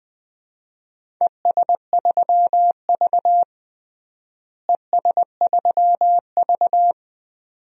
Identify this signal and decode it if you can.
Morse code, heard as a keyed tone, about 20 wpm, 700 Hz. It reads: ES3V ES3V
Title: DXCC-100 Call Signs - 4 Characters - Repeated 2x - 20wpm